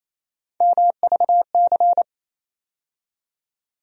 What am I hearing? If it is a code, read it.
Morse code, heard as a keyed tone, about 28 wpm, 700 Hz. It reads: MVC
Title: USA Call Sign Postfixes - 3 Letters 28wpm